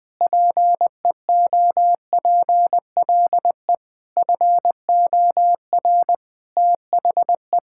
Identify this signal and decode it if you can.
Morse code, heard as a keyed tone, about 21 wpm, 700 Hz. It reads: PEOPLE FOR THE
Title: Sets of 9 Words - 20wpm